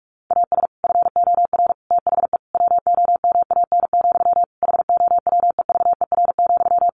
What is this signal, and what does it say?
Morse code, heard as a keyed tone, about 44 wpm, 700 Hz. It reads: AS FOR THE WOMAN, HOWEVER,
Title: Sentences from Top 1100 Words 45wpm